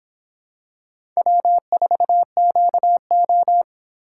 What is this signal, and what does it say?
Morse code, heard as a keyed tone, about 22 wpm, 700 Hz. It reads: W4QO